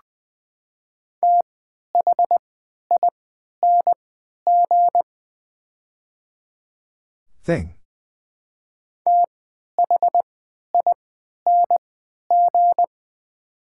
Morse code, heard as a keyed tone, about 21 wpm, spaced as for 12 wpm, 700 Hz. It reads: THING THING
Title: 3x Character Spacing - Top 200 Words - 20wpm